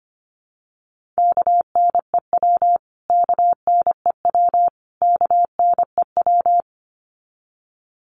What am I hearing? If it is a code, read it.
Morse code, heard as a keyed tone, about 25 wpm, 700 Hz. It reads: KNEW KNEW KNEW
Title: Top 500 Words Repeated 25wpm